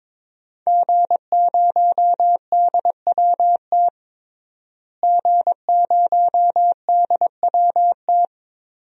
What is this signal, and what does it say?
Morse code, heard as a keyed tone, about 22 wpm, 700 Hz. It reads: G0DWT G0DWT
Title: DXCC-100 Call Signs - 5 Characters - Repeated 2x - 22wpm